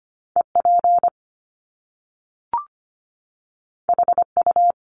Morse code, heard as a keyed tone, about 25 wpm, 700 Hz. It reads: E P H U
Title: Top Two Letter Combinations in Words 25wpm